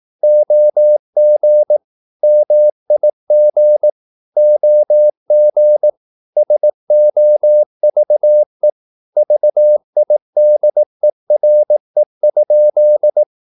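Morse code, heard as a keyed tone, about 18 wpm, 600 Hz. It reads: OG MIG OG SOVE VIDERE?
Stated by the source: Slow (QRS) Morse practice at 18 wpm, CW med Gnister.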